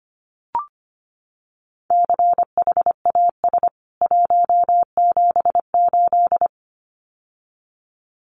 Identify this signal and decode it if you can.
Morse code, heard as a keyed tone, about 25 wpm, 700 Hz. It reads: CHAS 178
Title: CWT Exchange 25wpm